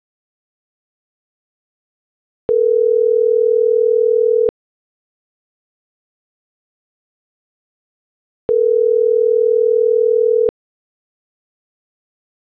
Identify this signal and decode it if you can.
Ringback tone (North America), 440 + 480 Hz.